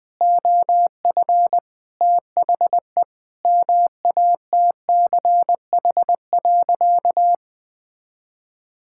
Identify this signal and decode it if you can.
Morse code, heard as a keyed tone, about 21 wpm, 700 Hz. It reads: OF THE MATCH.
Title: Sentences with 5-Word Phrases - 20wpm